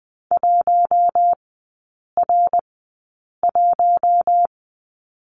Morse code, heard as a keyed tone, about 20 wpm, spaced as for 8 wpm, 700 Hz. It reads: 1R1
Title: 053 Lesson 10e - Letter r - Three Characters 20wpm